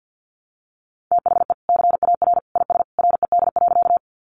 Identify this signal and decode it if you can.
Morse code, heard as a keyed tone, about 50 wpm, 700 Hz. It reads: THE CAR IS RED.
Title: Easy Phrases 50wpm